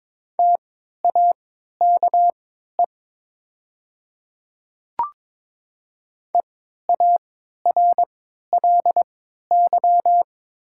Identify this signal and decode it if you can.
Morse code, heard as a keyed tone, about 22 wpm, spaced as for 12 wpm, 700 Hz. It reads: TAKE EARLY
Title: 3x Character Spacing - Top 500 Words - 22wpm